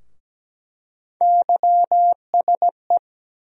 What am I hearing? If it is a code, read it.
Morse code, heard as a keyed tone, about 17 wpm, 700 Hz. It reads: YSE
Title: USA Call Sign Postfixes - 3 Letters 17wpm